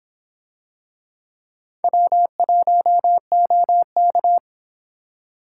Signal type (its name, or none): Morse code